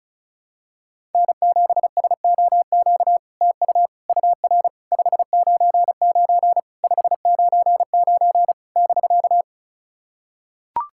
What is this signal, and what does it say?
Morse code, heard as a keyed tone, about 36 wpm, 700 Hz. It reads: N7SOQ TU UR 599 599 <BK>
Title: SOTA QSO Practice 35wpm